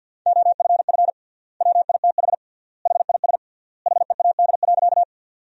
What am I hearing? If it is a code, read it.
Morse code, heard as a keyed tone, about 50 wpm, 700 Hz. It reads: OFF WITH HIS HEAD.
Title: 3x Word-Spacing - Sentences from Top 300 Words - 50wpm